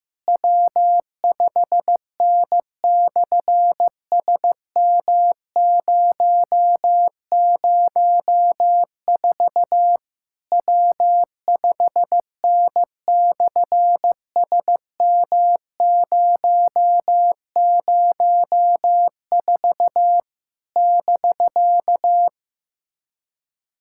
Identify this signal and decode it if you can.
Morse code, heard as a keyed tone, about 15 wpm, 700 Hz. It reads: W5N/SM004 W5N/SM004 <BK>
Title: SOTA QSO Practice 15wpm